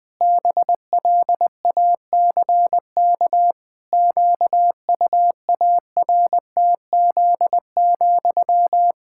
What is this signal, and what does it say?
Morse code, heard as a keyed tone, about 20 wpm, 700 Hz. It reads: BLACK QUARTZ,
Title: Pangram Challenge v2 - 20wpm